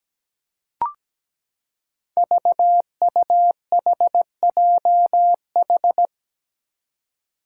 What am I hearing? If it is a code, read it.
Morse code, heard as a keyed tone, about 17 wpm, 700 Hz. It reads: VUHJH
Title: Five Challenging Letter-Number Combinations 17wpm